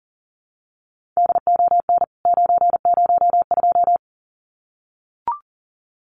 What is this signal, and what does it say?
Morse code, heard as a keyed tone, about 40 wpm, 700 Hz. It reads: DON 902
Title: CWT Exchange 40wpm